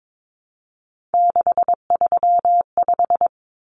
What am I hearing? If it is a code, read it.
Morse code, heard as a keyed tone, about 22 wpm, 700 Hz. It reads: 635
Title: Common Cut Numbers - 3-Digit 22wpm